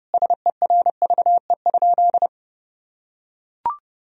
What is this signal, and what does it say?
Morse code, heard as a keyed tone, about 32 wpm, 700 Hz. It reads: SERVE?
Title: Sentences from Top 1000 Words - Review Part 2 - 30wpm